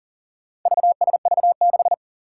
Morse code, heard as a keyed tone, about 40 wpm, 700 Hz. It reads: VSV6
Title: Four Challenging Letter-Number Combinations 40wpm